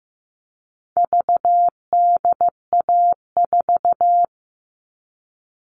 Morse code, heard as a keyed tone, about 15 wpm, 700 Hz. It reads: VDA4
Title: All Cut Numbers - 4-Digit 15wpm